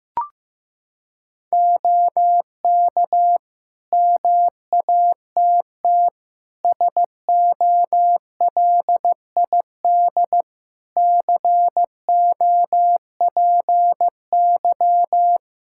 Morse code, heard as a keyed tone, about 16 wpm, 700 Hz. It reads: OK MATT SOLID COPY
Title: Practice QSO - Exchange 3 - More Info - 15wpm